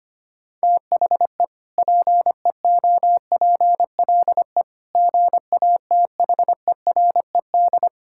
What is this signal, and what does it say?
Morse code, heard as a keyed tone, about 25 wpm, 700 Hz. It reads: THE PEOPLE GATHERED
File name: practice / 25wpm / Sentences with 2-Word Phrases - 25wpm